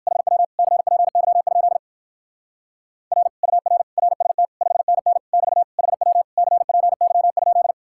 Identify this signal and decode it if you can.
Morse code, heard as a keyed tone, about 51 wpm, 700 Hz. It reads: HW CPY? R UR RST 5NN <BT> HW CPY?
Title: Speed-Racing Common QSO Phrases - 45wpm